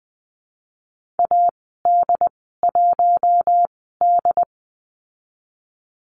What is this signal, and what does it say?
Morse code, heard as a keyed tone, about 20 wpm, spaced as for 14 wpm, 700 Hz. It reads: AD1D
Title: 088 Lesson 15g - Number 5 - Call Signs 20wpm